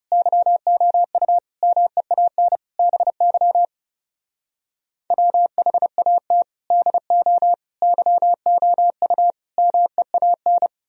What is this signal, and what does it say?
Morse code, heard as a keyed tone, about 37 wpm, 700 Hz. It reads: YOU MEAN BY WHAT DO YOU MEAN
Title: Speed-Racing - Sets of 5 Words 30wpm